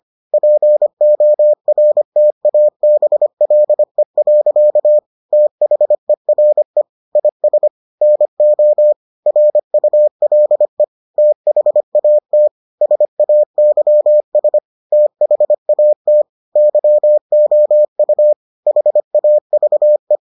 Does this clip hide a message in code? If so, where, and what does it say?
Yes: Morse code from 0.3 to 20.2 s, heard as a keyed tone, about 25 wpm, 600 Hz. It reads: PORTABLE. THERE IS NO RULE THAT SAYS THAT YOU HAVE